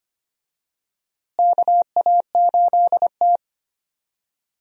Morse code, heard as a keyed tone, about 25 wpm, 700 Hz. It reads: KA8T
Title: USA Call Signs - 4 Characters - Repeated 3x - 25wpm